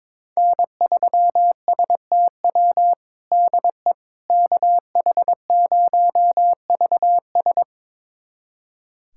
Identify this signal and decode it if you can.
Morse code, heard as a keyed tone, about 21 wpm, 700 Hz. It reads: N3STW DE KH0VS
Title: USA Callsign DE Callsign - 5 Characters - 22wpm